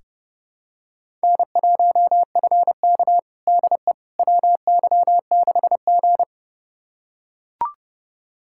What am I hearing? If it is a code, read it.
Morse code, heard as a keyed tone, about 30 wpm, 700 Hz. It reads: N1FK DE WY6G
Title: USA Callsign DE Callsign - 4 Characters - 30wpm